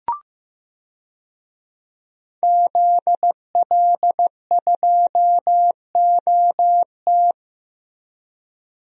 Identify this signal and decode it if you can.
Morse code, heard as a keyed tone, about 15 wpm, 700 Hz. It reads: ZL2OT
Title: DXCC-100 Call Signs - 5 Characters - 15wpm